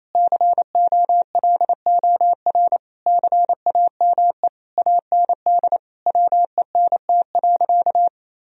Morse code, heard as a keyed tone, about 28 wpm, 700 Hz. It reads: COLOR CAME AND WENT.
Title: Sentences from Top 1000 Words - Review Part 2 - 28wpm